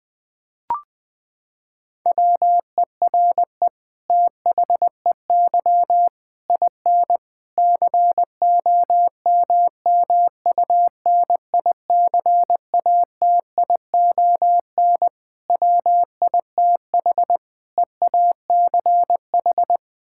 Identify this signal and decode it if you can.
Morse code, heard as a keyed tone, about 20 wpm, 700 Hz. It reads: WERE THEY IN COMMUNICATION WITH EACH
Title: Sentences from Top 1000 Words - Review Part 6 - 20wpm